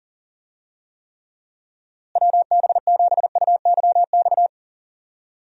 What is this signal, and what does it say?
Morse code, heard as a keyed tone, about 39 wpm, 700 Hz. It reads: WB7UYX
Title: USA Call Signs - 6 Characters - Repeated 3x - 40wpm